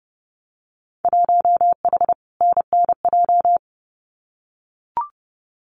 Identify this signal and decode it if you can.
Morse code, heard as a keyed tone, about 30 wpm, 700 Hz. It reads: 1H NNJ
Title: Winter Field Day Exchanges 30wpm